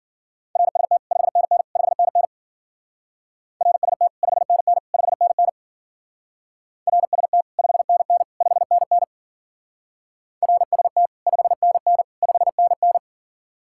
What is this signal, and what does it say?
Morse code, heard as a keyed tone, about 47 wpm, 700 Hz. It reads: RST 5NN 5NN RST 5NN 5NN RST 5NN 5NN RST 5NN 5NN
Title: Speed-Racing Common QSO Phrases - 40wpm